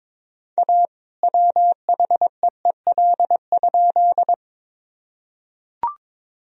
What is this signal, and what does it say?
Morse code, heard as a keyed tone, about 22 wpm, 700 Hz. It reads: A WHEEL?
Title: Sentences from Top 1000 Words - Review Part 3 - 22wpm